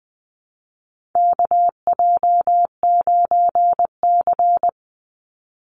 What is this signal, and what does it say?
Morse code, heard as a keyed tone, about 20 wpm, 700 Hz. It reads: KJ9C
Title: Call Signs Repeated 3x - 20wpm